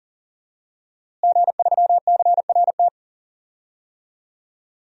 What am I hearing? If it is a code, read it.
Morse code, heard as a keyed tone, about 40 wpm, 700 Hz. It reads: G3CRT
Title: DXCC-100 Call Signs - 5 Characters - Repeated 2x - 40wpm